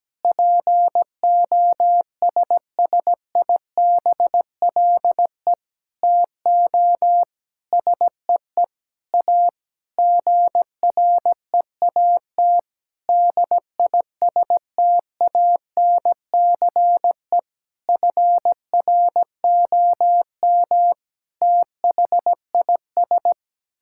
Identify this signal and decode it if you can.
Morse code, heard as a keyed tone, about 17 wpm, 700 Hz. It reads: POSSIBLE TO SEE A GREAT DISTANCE FROM THIS
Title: Sentences from Top 1600 Words 17wpm